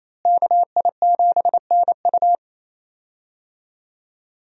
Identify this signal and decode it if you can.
Morse code, heard as a keyed tone, about 28 wpm, 700 Hz. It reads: KI7NU
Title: USA Call Signs - 5 Characters - Repeated 3x - 28wpm